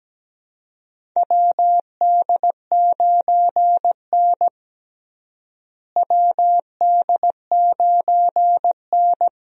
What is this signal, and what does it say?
Morse code, heard as a keyed tone, about 17 wpm, 700 Hz. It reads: WD9N WD9N
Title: USA Call Signs - 4 Characters - Repeated 3x - 17wpm